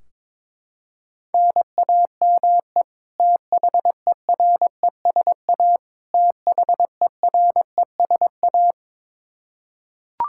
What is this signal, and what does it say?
Morse code, heard as a keyed tone, about 22 wpm, 700 Hz. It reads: NAME THERESA THERESA
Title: Practice QSO - Exchange 1 - Beginning - 22wpm